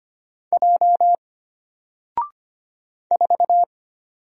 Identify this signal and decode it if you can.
Morse code, heard as a keyed tone, about 25 wpm, 700 Hz. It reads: J 4
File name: Single-Letter-Number-25wpm